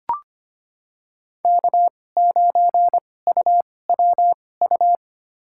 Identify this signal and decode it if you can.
Morse code, heard as a keyed tone, about 25 wpm, spaced as for 18 wpm, 700 Hz. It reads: K9UWU
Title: USA Call Signs - 5 Characters - 2x Character Spacing - 25wpm